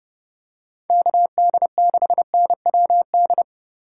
Morse code, heard as a keyed tone, about 29 wpm, 700 Hz. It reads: KD6NWD